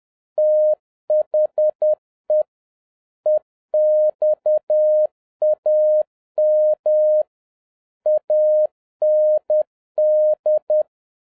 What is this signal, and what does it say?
Morse code, heard as a keyed tone, about 10 wpm, 610 Hz. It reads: THE EXAM AND